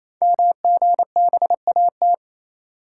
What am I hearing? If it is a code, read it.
Morse code, heard as a keyed tone, about 27 wpm, 700 Hz. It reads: MGBAT
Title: Code Groups - 5 Letter:Numbers - 28wpm